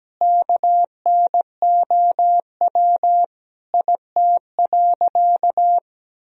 Morse code, heard as a keyed tone, about 17 wpm, 700 Hz. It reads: KNOW IT.
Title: Sentences from Top 1000 Words - Review Part 2 - 17wpm